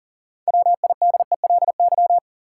Morse code, heard as a keyed tone, about 40 wpm, 700 Hz. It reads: WIDELY
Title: Sentences from Top 1200 Words - New Word Drill - 40wpm